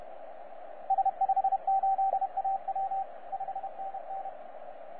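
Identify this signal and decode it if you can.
Morse code, heard as a keyed tone, about 31 wpm, 780 Hz, with radio noise. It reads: S58AW 5NN
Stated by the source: 9U4U on 15 metre CW, DX station recording by DX World.net.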